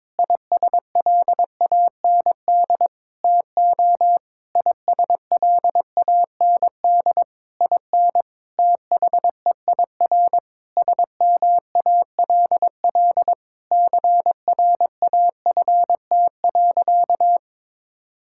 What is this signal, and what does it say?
Morse code, heard as a keyed tone, about 22 wpm, 700 Hz. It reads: ISLAND TO ISLAND IN THEIR SMALL CRAFT.
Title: Sentences from Top 1000 Words - Review Part 6 - 22wpm